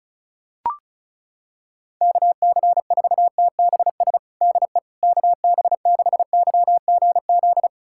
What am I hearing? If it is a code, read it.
Morse code, heard as a keyed tone, about 36 wpm, 690 Hz. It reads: KC4TBS DE KB6YGZ
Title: USA Callsign DE Callsign - 6 Characters - 35wpm